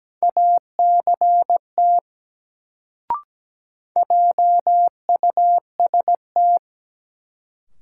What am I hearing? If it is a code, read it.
Morse code, heard as a keyed tone, about 17 wpm, 700 Hz. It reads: ACT JUST
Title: Sentences from Top 200 Words - New Word Drill - 17wpm